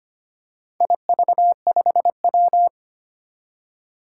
Morse code, heard as a keyed tone, about 25 wpm, 700 Hz. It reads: IV5W
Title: Four Challenging Letter-Number Combinations 25wpm